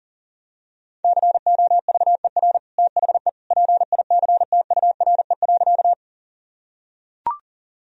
Morse code, heard as a keyed tone, about 40 wpm, 700 Hz. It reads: COVER THE PICTURE.
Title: Sentences from Top 1000 Words - Review Part 1 - 40wpm